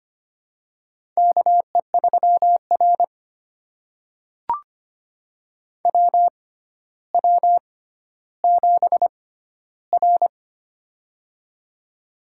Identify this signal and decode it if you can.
Morse code, heard as a keyed tone, about 25 wpm, spaced as for 13 wpm, 700 Hz. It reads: KE3R WW7R
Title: USA Call Signs - 4 Characters - 6x Character Spacing - 25wpm